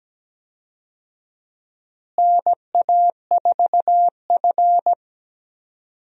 Morse code, heard as a keyed tone, about 17 wpm, 700 Hz. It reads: NA4F
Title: USA Call Signs - 4 Characters - Repeated 2x - 17wpm